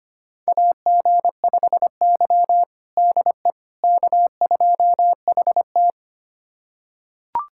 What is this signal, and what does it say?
Morse code, heard as a keyed tone, about 25 wpm, 700 Hz. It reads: AG5Y DE K2HT
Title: USA Callsign DE Callsign - 4 Characters - 25wpm